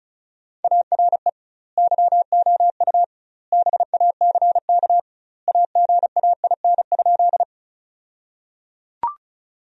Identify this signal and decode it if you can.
Morse code, heard as a keyed tone, about 36 wpm, 700 Hz. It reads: ARE YOU BACK AGAIN?
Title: 2x Word-Spacing - Sentences from Top 200 Words - 35wpm